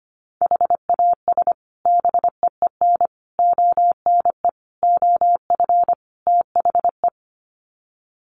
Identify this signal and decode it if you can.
Morse code, heard as a keyed tone, about 25 wpm, 700 Hz. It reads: HAS BEEN ONE OF THE
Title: Sets of 5 Words Encore 25wpm